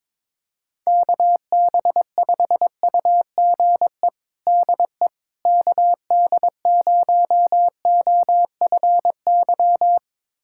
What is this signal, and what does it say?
Morse code, heard as a keyed tone, about 22 wpm, 700 Hz. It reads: KB5UGE DE KD0OFY
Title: USA Callsign DE Callsign 22wpm